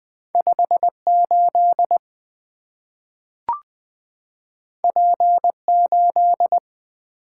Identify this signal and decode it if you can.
Morse code, heard as a keyed tone, about 20 wpm, 700 Hz. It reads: 58 P8
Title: Two Letter-Number Combinations - 20wpm